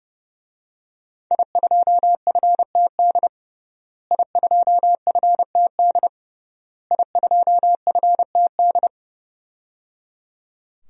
Morse code, heard as a keyed tone, about 30 wpm, 700 Hz. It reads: I2FTD I2FTD I2FTD
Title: DXCC-100 Call Signs - 5 Characters - Repeated 3x - 30wpm